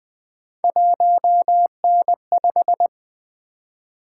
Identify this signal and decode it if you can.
Morse code, heard as a keyed tone, about 20 wpm, 700 Hz. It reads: 1N5
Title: Three Letter-Number Combinations - 20wpm